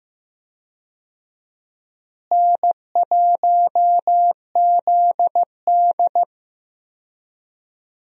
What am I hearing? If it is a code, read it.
Morse code, heard as a keyed tone, about 15 wpm, 700 Hz. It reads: N1ZD
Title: USA Call Signs - 4 Characters - Repeated 2x - 15wpm